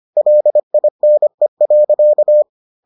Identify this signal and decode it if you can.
Morse code, heard as a keyed tone, about 25 wpm, 600 Hz. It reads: LINE.